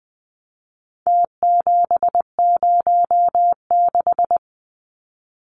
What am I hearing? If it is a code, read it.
Morse code, heard as a keyed tone, about 20 wpm, 700 Hz. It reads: T706
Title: Numbers - Contest Style - 4-Digit 20wpm